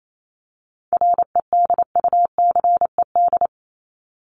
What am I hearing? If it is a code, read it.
Morse code, heard as a keyed tone, about 28 wpm, 700 Hz. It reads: REDUCED
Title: ED Suffix - Top 100 Words with Base Words 28wpm